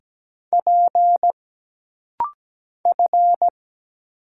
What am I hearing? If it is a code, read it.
Morse code, heard as a keyed tone, about 17 wpm, 700 Hz. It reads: P F
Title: Single Letters - Rapid-Fire 17wpm